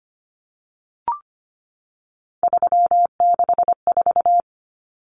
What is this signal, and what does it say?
Morse code, heard as a keyed tone, about 25 wpm, 690 Hz. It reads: 364
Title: Numbers - 3-Digit 25wpm